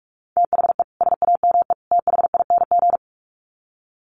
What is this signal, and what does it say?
Morse code, heard as a keyed tone, about 44 wpm, 700 Hz. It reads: THE SAME THING
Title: Sets of 3 Words 45wpm